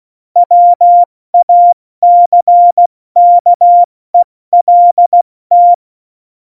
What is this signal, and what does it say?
Morse code, heard as a keyed tone, about 16 wpm, spaced as for 14 wpm, 700 Hz. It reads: WACKELT
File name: Morsekurs / Witze / MP3, Witz 24 April 2018, 70 F80